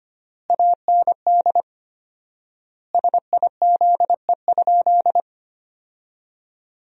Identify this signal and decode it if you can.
Morse code, heard as a keyed tone, about 25 wpm, 700 Hz. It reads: AND SIZE?
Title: 4x Word-Spacing - Sentences from Top 500 Words - 25wpm